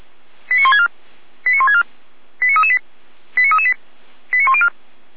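Five-tone selective call (ZVEI), heard as quick runs of five tones, tones of about 70 ms. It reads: 89164 89164 89338 89338 89104
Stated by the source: ZVEI five-tone call 89701 received by SDR.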